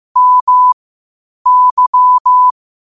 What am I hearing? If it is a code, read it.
Morse code, heard as a keyed tone, about 15 wpm, spaced as for 8 wpm, 1000 Hz. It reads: MY
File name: qotdfiles / 8wpm